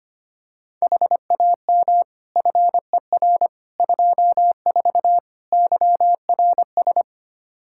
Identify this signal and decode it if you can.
Morse code, heard as a keyed tone, about 25 wpm, 700 Hz. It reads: HAM FER 24 YRS
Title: Practice QSO - Exchange 3 - More Info - 25wpm